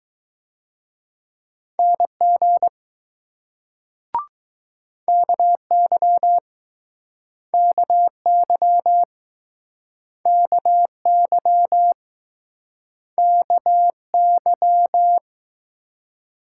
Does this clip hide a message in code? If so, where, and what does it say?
Yes: Morse code from 1.8 to 15.2 s, heard as a keyed tone, about 20 wpm, 700 Hz. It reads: NG KY KY KY KY